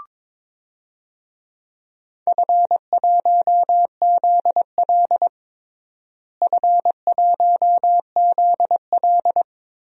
Morse code, heard as a keyed tone, about 22 wpm, 700 Hz. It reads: F1ZL F1ZL
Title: DXCC-100 Call Signs - 4 Characters - Repeated 2x - 22wpm